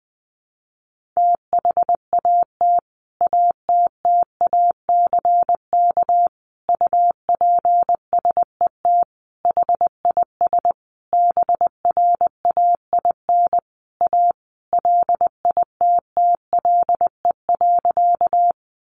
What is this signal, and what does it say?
Morse code, heard as a keyed tone, about 20 wpm, 700 Hz. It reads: THAT ATTACK UPSET HIS BRAIN A LITTLE.
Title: Sentences from Top 1500 Words 20wpm